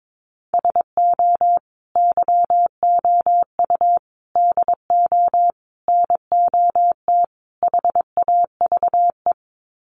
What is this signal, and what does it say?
Morse code, heard as a keyed tone, about 22 wpm, 700 Hz. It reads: SO YOU DO NOT HAVE
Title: Sets of 5 Words Encore 22wpm